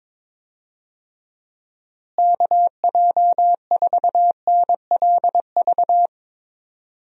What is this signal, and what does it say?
Morse code, heard as a keyed tone, about 21 wpm, 700 Hz. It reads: KJ4NLV
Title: USA Call Signs - 6 Characters - Repeated 2x - 22wpm